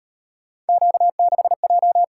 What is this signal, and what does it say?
Morse code, heard as a keyed tone, about 32 wpm, 700 Hz. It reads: Q6J